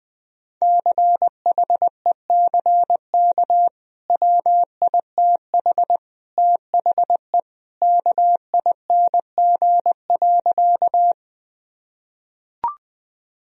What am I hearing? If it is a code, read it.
Morse code, heard as a keyed tone, about 20 wpm, 700 Hz. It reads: CHECK WITH THE KING.
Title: Sentences from Top 1000 Words - Review Part 3 - 20wpm